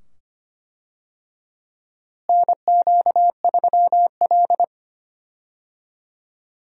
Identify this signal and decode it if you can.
Morse code, heard as a keyed tone, about 25 wpm, 700 Hz. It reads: NQ3L
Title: USA Call Signs - 4 Characters - Repeated 2x - 25wpm